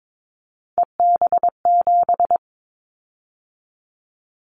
Morse code, heard as a keyed tone, about 22 wpm, 700 Hz. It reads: EB7
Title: Callsign Prefixes for Top 25 DXCC - 22wpm